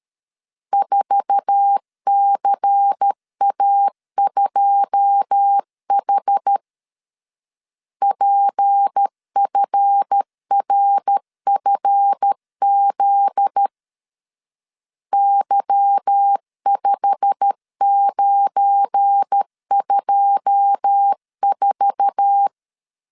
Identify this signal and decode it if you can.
Morse code, heard as a keyed tone, about 13 wpm, 800 Hz. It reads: 4CA2H PFRFZ Y5924
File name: CW / DET / 1 9 1500C S C 12wpm 3811